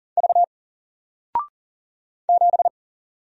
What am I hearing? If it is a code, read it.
Morse code, heard as a keyed tone, about 41 wpm, 700 Hz. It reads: V 7
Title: Single Character (Letter:Number:Punctuation:Prosign) 40wpm